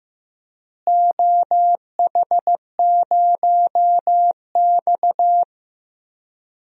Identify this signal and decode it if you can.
Morse code, heard as a keyed tone, about 15 wpm, 700 Hz. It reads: OH0X
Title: DXCC-100 Call Signs - 4 Characters - Repeated 3x - 15wpm